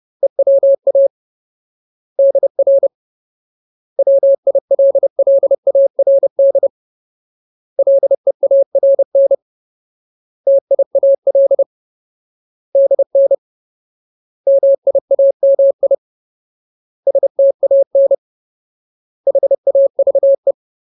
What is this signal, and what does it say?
Morse code, heard as a keyed tone, about 30 wpm, 550 Hz. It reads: EWA DR WILLARD LEARN TIAL DN MIAMI STAN HAVE
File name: Ultimate Word List / 30WPM Ultimate